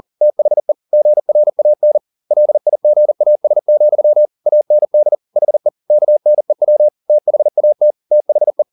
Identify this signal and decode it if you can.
Morse code, heard as a keyed tone, about 40 wpm, 600 Hz. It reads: THE GRAN LIGAS, AND HE KNEW THAT THE